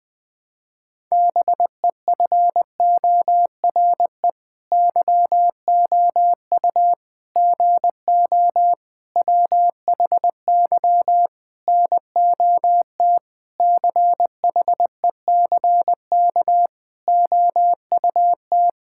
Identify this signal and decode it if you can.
Morse code, heard as a keyed tone, about 20 wpm, 700 Hz. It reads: BEFORE YOU GO WHY NOT CHECK OUT
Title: Sentences with 9-Word Phrases - Spoken Beginning - 20wpm